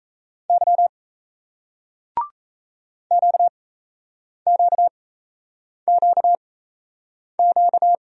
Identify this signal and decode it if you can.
Morse code, heard as a keyed tone, about 37 wpm, spaced as for 8 wpm, 700 Hz. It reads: Y QQQQ